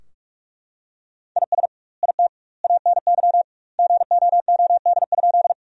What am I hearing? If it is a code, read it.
Morse code, heard as a keyed tone, about 41 wpm, 700 Hz. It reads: IS IT ANY GOOD?
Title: 2x Word-Spacing - Sentences from Top 200 Words - 45wpm